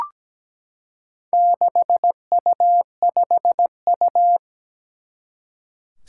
Morse code, heard as a keyed tone, about 17 wpm, 700 Hz. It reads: 6U5U
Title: Four Challenging Letter-Number Combinations 17wpm